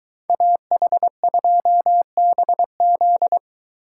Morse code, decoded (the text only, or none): AH2BZ